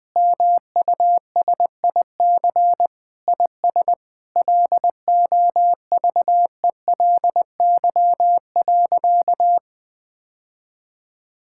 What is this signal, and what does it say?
Morse code, heard as a keyed tone, about 20 wpm, 700 Hz. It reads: MUSIC IS LOVELY.
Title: Sentences with 3-Word Phrases - 20wpm